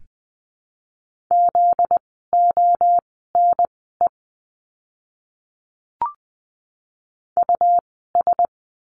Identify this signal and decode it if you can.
Morse code, heard as a keyed tone, about 20 wpm, spaced as for 14 wpm, 700 Hz. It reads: ZONE US
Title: 244 Lesson 39f - Letter z - Words 20wpm